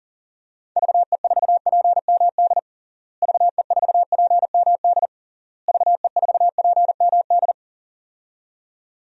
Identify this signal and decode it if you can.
Morse code, heard as a keyed tone, about 40 wpm, 700 Hz. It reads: VE4PMD VE4PMD VE4PMD
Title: DXCC-100 Call Signs - 6 Characters - Repeated 3x - 40wpm